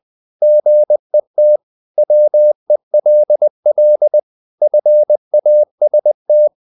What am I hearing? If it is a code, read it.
Morse code, heard as a keyed tone, about 20 wpm, 600 Hz. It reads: GET WELL FAST
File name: the old man and the sea part 2 20wpm